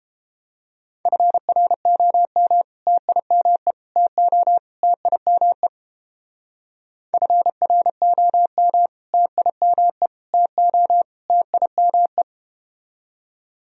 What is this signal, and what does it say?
Morse code, heard as a keyed tone, about 31 wpm, 700 Hz. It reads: FROM TIME TO TIME FROM TIME TO TIME